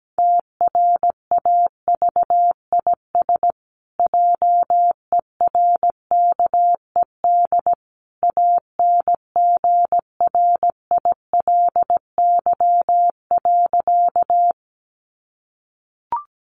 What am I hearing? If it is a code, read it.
Morse code, heard as a keyed tone, about 17 wpm, 700 Hz. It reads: TRAVIS JERKED ANGRILY.
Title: A Sound Of Thunder 17wpm